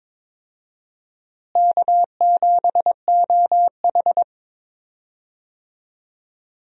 Morse code, heard as a keyed tone, about 22 wpm, 700 Hz. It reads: K7OH
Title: USA Call Signs - 4 Characters - Repeated 3x - 22wpm